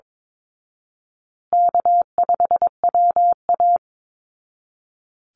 Morse code, heard as a keyed tone, about 22 wpm, 700 Hz. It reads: K5WA
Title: Easy Call Signs Repeated 2x - 22wpm